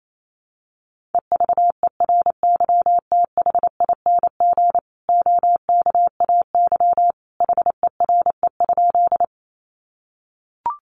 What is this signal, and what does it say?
Morse code, heard as a keyed tone, about 28 wpm, 700 Hz. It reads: EVERYTHING OKAY HERE?
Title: A Sound Of Thunder 28wpm